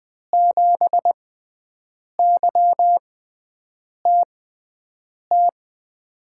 Morse code, heard as a keyed tone, about 20 wpm, spaced as for 7 wpm, 700 Hz. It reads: 7YTT
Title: USA Call Signs - 6 Characters - 6x Character Spacing - 20wpm